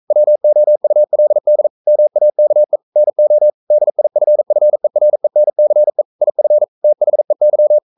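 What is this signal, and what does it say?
Morse code, heard as a keyed tone, about 42 wpm, 600 Hz. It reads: WOULD MAKE NO DIFFERENCE IF THEY
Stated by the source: Morse code sending of an English novel at 42 wpm.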